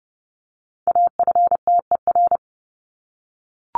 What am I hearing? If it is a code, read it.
Morse code, heard as a keyed tone, about 30 wpm, 700 Hz. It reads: AFTER
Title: Top 200 Words 30wpm